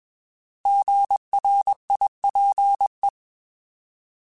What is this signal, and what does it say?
Morse code, heard as a keyed tone, about 21 wpm, 800 Hz. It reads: GRIPE
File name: Alphabetical / Alphabetical mp3s / Gs @20